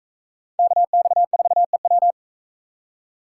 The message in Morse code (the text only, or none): KX4EW